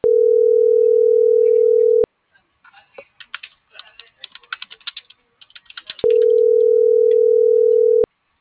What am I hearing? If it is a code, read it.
Ringback tone (North America), 440 + 480 Hz.